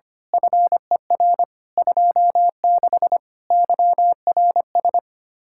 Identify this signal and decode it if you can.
Morse code, heard as a keyed tone, about 25 wpm, 700 Hz. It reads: FER 26 YRS
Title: Speed-Racing Common QSO Phrases - 25wpm